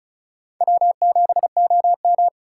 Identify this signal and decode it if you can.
Morse code, heard as a keyed tone, about 36 wpm, 700 Hz. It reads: W7OM
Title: USA Call Signs - 4 Characters - 2x Character Spacing - 35wpm